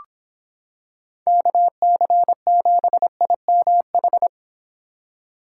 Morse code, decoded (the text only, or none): KC7IMH